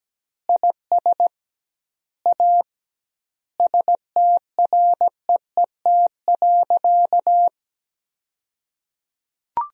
Morse code, heard as a keyed tone, about 17 wpm, 700 Hz. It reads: IS A STREET.
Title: 2x Word-Spacing - Sentences from Top 500 Words - 17wpm